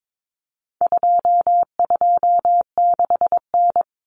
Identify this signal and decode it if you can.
Morse code, heard as a keyed tone, about 22 wpm, 700 Hz. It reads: 226N
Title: Common Cut Numbers - 4-Digit 22wpm